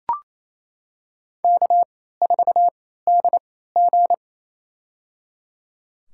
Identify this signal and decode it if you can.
Morse code, heard as a keyed tone, about 28 wpm, spaced as for 16 wpm, 700 Hz. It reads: K4DG
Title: USA Call Signs - 4 Characters - 3x Character Spacing - 28wpm